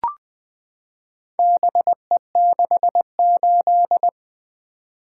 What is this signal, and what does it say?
Morse code, heard as a keyed tone, about 20 wpm, 700 Hz. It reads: BE68
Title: Four Letter-Number Combinations - 20wpm